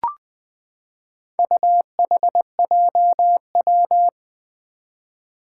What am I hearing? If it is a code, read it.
Morse code, heard as a keyed tone, about 20 wpm, 700 Hz. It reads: UHJW
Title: Four Challenging Letter-Number Combinations 20wpm